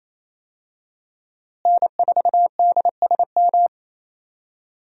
Morse code, heard as a keyed tone, about 27 wpm, 700 Hz. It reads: N4DSM
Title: USA Call Signs - 5 Characters - Repeated 3x - 28wpm